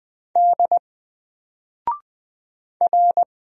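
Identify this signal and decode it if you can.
Morse code, heard as a keyed tone, about 20 wpm, 700 Hz. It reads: D R
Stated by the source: Single letter and number Morse drill at 20 wpm, from N4LCD.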